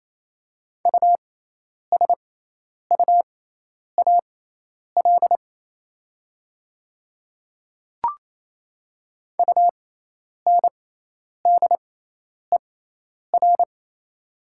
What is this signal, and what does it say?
Morse code, heard as a keyed tone, about 28 wpm, spaced as for 10 wpm, 700 Hz. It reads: USUAL UNDER
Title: 6x Character Spacing - Top 500 Words - 28wpm